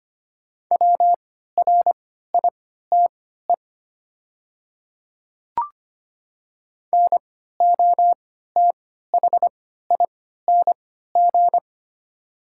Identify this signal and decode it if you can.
Morse code, heard as a keyed tone, about 25 wpm, spaced as for 14 wpm, 690 Hz. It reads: WRITE NOTHING
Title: 3x Character Spacing - Top 500 Words - 25wpm